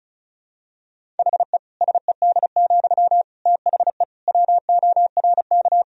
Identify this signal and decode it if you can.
Morse code, heard as a keyed tone, about 36 wpm, 700 Hz. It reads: HE SED, THE WORK
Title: Sentences from Top 400 Words and Abbreviations 35wpm